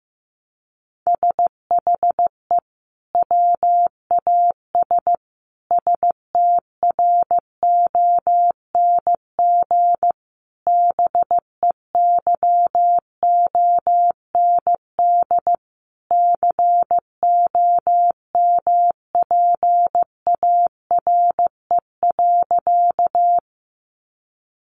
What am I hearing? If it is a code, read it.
Morse code, heard as a keyed tone, about 15 wpm, 700 Hz. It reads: SHE WAS STRONG BEYOND COMPARE.
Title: Sentences from Top 1300 Words 15wpm